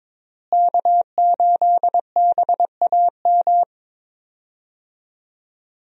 Morse code, heard as a keyed tone, about 22 wpm, 700 Hz. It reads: K8BAM
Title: USA Call Signs - 5 Characters - Repeated 2x - 22wpm